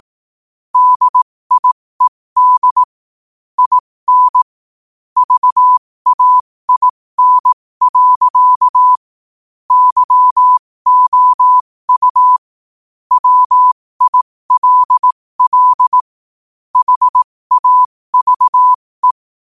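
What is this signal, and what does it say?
Morse code, heard as a keyed tone, about 18 wpm, spaced as for 15 wpm, 1000 Hz. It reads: DIED IN VAIN. YOU WILL HAVE
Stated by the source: Morse quote-of-the-day practice at 15 WPM from Morse Resource.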